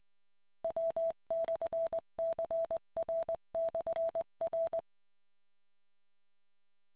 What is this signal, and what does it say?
Morse code, heard as a keyed tone, about 20 wpm, 660 Hz. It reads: W/CR/R